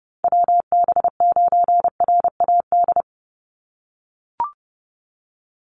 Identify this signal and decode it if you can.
Morse code, heard as a keyed tone, about 30 wpm, 700 Hz. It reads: WB9RAD
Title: USA Call Signs 30wpm